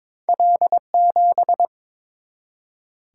Morse code, heard as a keyed tone, about 22 wpm, 700 Hz. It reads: L7
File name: practice / 22wpm / Two Letter-Number Combinations - 22wpm